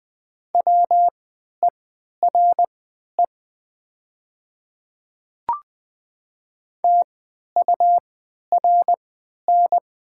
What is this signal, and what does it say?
Morse code, heard as a keyed tone, about 20 wpm, spaced as for 11 wpm, 700 Hz. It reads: WERE TURN
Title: 3x Character Spacing - Top 300 Words - 20wpm